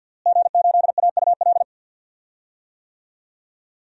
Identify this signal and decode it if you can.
Morse code, heard as a keyed tone, about 50 wpm, 700 Hz. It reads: G8AUL
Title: DXCC-100 Call Signs - 5 Characters - Repeated 2x - 50wpm